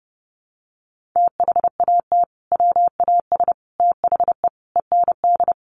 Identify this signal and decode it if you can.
Morse code, heard as a keyed tone, about 30 wpm, 700 Hz. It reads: THAT WAS THE END
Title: Sentences from Top 1400 Words 30wpm